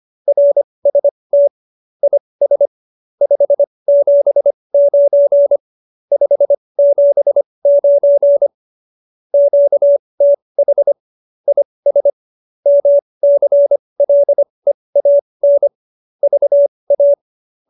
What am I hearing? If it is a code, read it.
Morse code, heard as a keyed tone, about 25 wpm, spaced as for 20 wpm, 570 Hz. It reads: RST IS 579 579 QTH IS MCLEAN VA
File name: QSO307 20